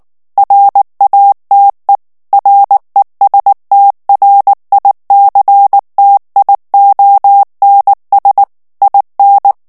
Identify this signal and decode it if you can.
Morse code, heard as a keyed tone, about 19 wpm, 800 Hz. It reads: RATE RESTRICTIONS IN
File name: ANGLAIS texte clair 3 vitesse 18 mots minute